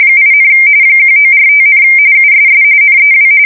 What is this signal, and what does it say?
RTTY, about 45.45 baud. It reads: 8, 1945, USS NEW
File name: fsk170 / USS-New-Jersey